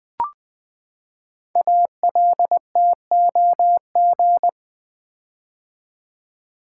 Morse code, heard as a keyed tone, about 20 wpm, 700 Hz. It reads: ALTOG
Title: Build A Word - 5 Letters At A Time - 10-Letter Words - 20wpm